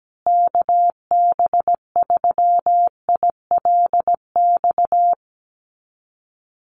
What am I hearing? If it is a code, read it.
Morse code, heard as a keyed tone, about 17 wpm, 700 Hz. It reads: KB3ILX
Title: USA Call Signs Repeated 3x 17wpm